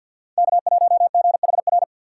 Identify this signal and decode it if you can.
Morse code, heard as a keyed tone, about 47 wpm, 700 Hz. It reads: K1GHR
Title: USA Call Signs - 5 Characters - 4x Character Spacing - 50wpm